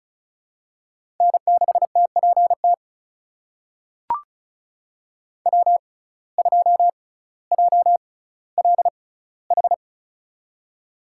Morse code, heard as a keyed tone, about 36 wpm, spaced as for 18 wpm, 700 Hz. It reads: N6TPT W2JLH
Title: USA Call Signs - 5 Characters - 6x Character Spacing - 35wpm